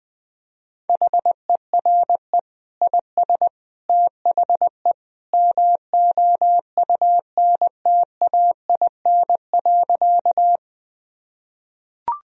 Morse code, heard as a keyed tone, about 20 wpm, 700 Hz. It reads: HERE IS THE MOUNTAIN.
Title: Sentences from Top 1000 Words - Review Part 1 - 20wpm